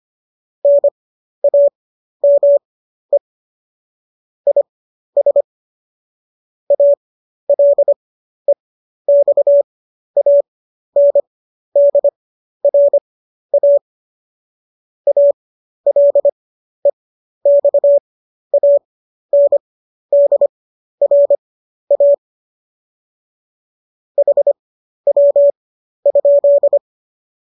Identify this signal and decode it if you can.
Morse code, heard as a keyed tone, about 25 wpm, spaced as for 12 wpm, 570 Hz. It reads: NAME IS ALEXANDRA ALEXANDRA HW?
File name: qso012 12